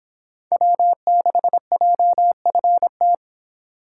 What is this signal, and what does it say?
Morse code, heard as a keyed tone, about 26 wpm, 700 Hz. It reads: W6JFT